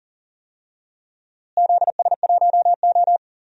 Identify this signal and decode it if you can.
Morse code, heard as a keyed tone, about 39 wpm, 700 Hz. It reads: ZS1O
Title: DXCC-100 Call Signs - 4 Characters - Repeated 3x - 40wpm